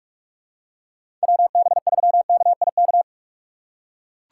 Morse code, heard as a keyed tone, about 43 wpm, 700 Hz. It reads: WB3KIK
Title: USA Call Signs - 6 Characters - 2x Character Spacing - 45wpm